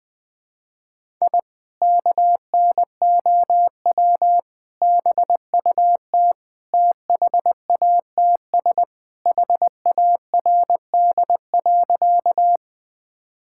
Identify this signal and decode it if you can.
Morse code, heard as a keyed tone, about 20 wpm, 700 Hz. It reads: I KNOW BUT THATS HARD.
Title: Sentences with 2-Word Phrases - 20wpm